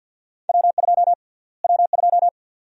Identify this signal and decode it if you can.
Morse code, heard as a keyed tone, about 51 wpm, 700 Hz. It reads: W2 W2
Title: USA Call Sign Prefixes - Repeated 2x - 50wpm